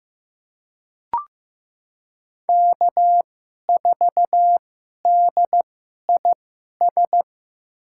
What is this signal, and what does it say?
Morse code, heard as a keyed tone, about 15 wpm, spaced as for 11 wpm, 700 Hz. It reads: K4DIS
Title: USA Call Signs - 5 Characters - 2x Character Spacing - 15wpm